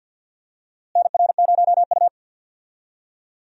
Morse code, heard as a keyed tone, about 52 wpm, 700 Hz. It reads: NR0U
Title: USA Call Signs - 4 Characters - Repeated 3x - 50wpm